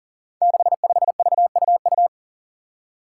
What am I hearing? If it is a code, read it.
Morse code, heard as a keyed tone, about 40 wpm, 700 Hz. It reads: 65VUU
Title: Five Challenging Letter-Number Combinations 40wpm